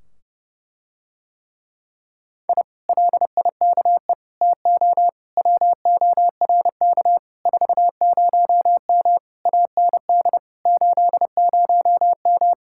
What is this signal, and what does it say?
Morse code, heard as a keyed tone, about 30 wpm, 700 Hz. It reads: I LIKE TO WORK 40M AND 80M